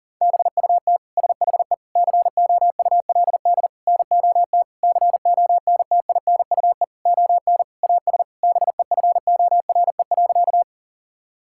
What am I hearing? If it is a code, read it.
Morse code, heard as a keyed tone, about 40 wpm, 700 Hz. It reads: BUT SHE COULD NOT CONTINUE ON AS BEFORE.
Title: Sentences from Top 1000 Words - Review Part 6 - 40wpm